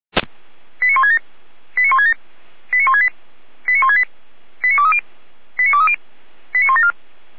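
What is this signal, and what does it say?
Five-tone selective call (ZVEI), heard as quick runs of five tones, tones of about 70 ms. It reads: 89167 89167 89168 89168 89230 89230 89174
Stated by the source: ZVEI five-tone selective call 89163 received by SDR.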